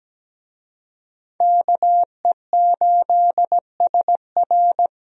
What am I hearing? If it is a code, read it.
Morse code, heard as a keyed tone, about 17 wpm, 700 Hz. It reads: KE8SR
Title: USA Call Signs - 5 Characters - Repeated 3x - 17wpm